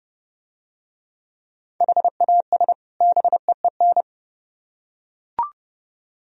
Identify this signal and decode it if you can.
Morse code, heard as a keyed tone, about 31 wpm, 700 Hz. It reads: HAS BEEN